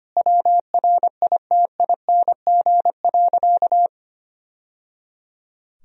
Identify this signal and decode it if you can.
Morse code, heard as a keyed tone, about 25 wpm, 700 Hz. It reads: WRITING.
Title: Sentences from Top 1000 Words - Review Part 1 - 25wpm